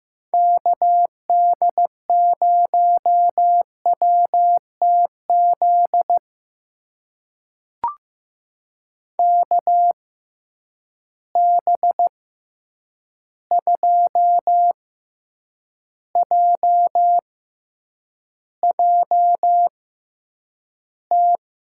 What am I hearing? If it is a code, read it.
Morse code, heard as a keyed tone, about 15 wpm, spaced as for 8 wpm, 700 Hz. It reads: KD0WTZ KB2JJT
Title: USA Call Signs - 6 Characters - 6x Character Spacing - 15wpm